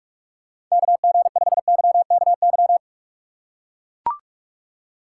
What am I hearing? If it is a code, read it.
Morse code, heard as a keyed tone, about 45 wpm, 700 Hz. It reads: KG5YKY